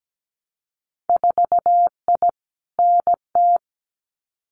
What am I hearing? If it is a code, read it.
Morse code, heard as a keyed tone, about 17 wpm, 700 Hz. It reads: 4I NT
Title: Winter Field Day Exchanges Encore 17wpm